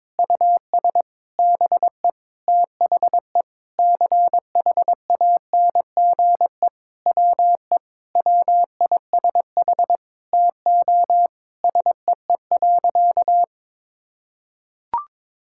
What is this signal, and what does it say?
Morse code, heard as a keyed tone, about 22 wpm, 700 Hz. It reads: US BE THE CHANGE WE WISH TO SEE.
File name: Sentences with 7-Word Phrases - 22wpm